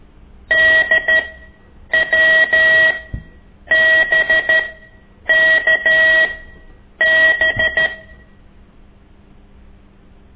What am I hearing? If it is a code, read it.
Morse code, heard as a keyed tone, about 13 wpm, spaced as for 8 wpm, 1900 Hz. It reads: DWBKB
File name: ik1weg2 / CORSI CW